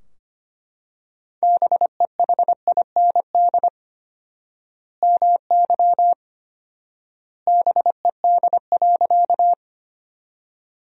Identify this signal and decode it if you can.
Morse code, heard as a keyed tone, about 25 wpm, 700 Hz. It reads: BEHIND MY BED.
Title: 4x Word-Spacing - Sentences from Top 500 Words - 25wpm